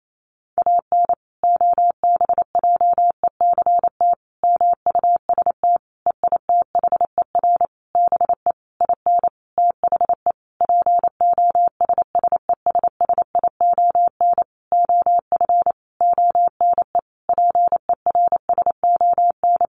Morse code, heard as a keyed tone, about 28 wpm, 700 Hz. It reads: AN OBJECT MUST EITHER BE IN THE POSSESSION OF ONE PERSON
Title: Sentences from Top 1400 Words 28wpm